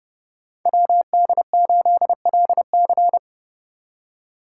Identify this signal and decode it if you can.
Morse code, heard as a keyed tone, about 34 wpm, 700 Hz. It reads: WD8LC